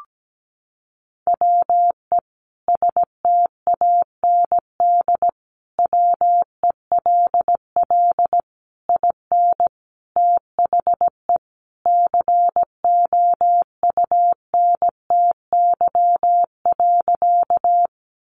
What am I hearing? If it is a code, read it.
Morse code, heard as a keyed tone, about 17 wpm, 700 Hz. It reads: WE STAND WELL IN THE COUNTY.
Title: Sentences from Top 1200 Words 17wpm